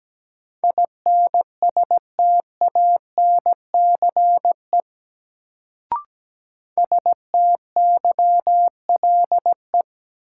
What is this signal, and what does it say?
Morse code, heard as a keyed tone, about 17 wpm, 700 Hz. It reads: INSTANCE STYLE
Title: Sentences from Top 800 Words - New Word Drill - 17wpm